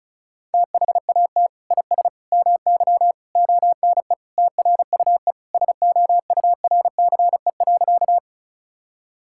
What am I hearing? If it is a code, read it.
Morse code, heard as a keyed tone, about 36 wpm, 700 Hz. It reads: THAT IS MY ONE TRUE SOURCE.
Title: Sentences from Top 1000 Words - Review Part 5 - 35wpm